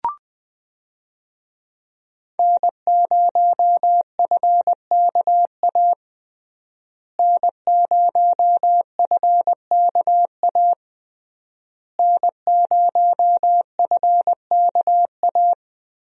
Morse code, heard as a keyed tone, about 21 wpm, 700 Hz. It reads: N0FKA N0FKA N0FKA
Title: USA Call Signs - 5 Characters - Repeated 3x - 20wpm